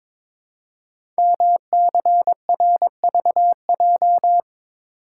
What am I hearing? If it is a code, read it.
Morse code, heard as a keyed tone, about 22 wpm, 700 Hz. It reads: MCRVJ
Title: Code Groups - 5 Letters - 22wpm